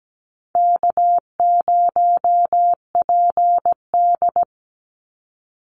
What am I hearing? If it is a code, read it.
Morse code, heard as a keyed tone, about 17 wpm, 700 Hz. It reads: K0PD
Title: Easy Call Signs Repeated 2x - 17wpm